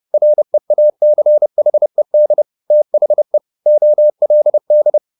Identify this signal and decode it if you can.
Morse code, heard as a keyed tone, about 30 wpm, 600 Hz. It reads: REACHED THE OLD
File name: the old man and the sea / the old man and the sea 1 30wpm